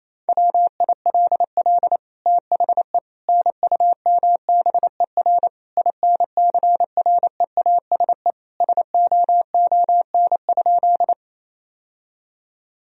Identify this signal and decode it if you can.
Morse code, heard as a keyed tone, about 28 wpm, 700 Hz. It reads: WILL THE NUMBER INCREASE SOON?
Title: Sentences from Top 1000 Words - Review Part 4 - 28wpm